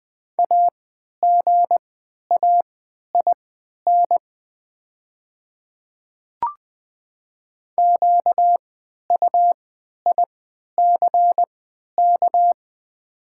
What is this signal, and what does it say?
Morse code, heard as a keyed tone, about 20 wpm, spaced as for 11 wpm, 700 Hz. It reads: AGAIN QUICK
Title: 3x Character Spacing - Top 500 Words - 20wpm